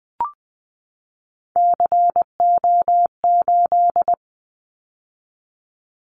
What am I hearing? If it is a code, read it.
Morse code, heard as a keyed tone, about 20 wpm, 700 Hz. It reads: CO8
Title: Callsign Prefixes for Top 100 DXCC - 20wpm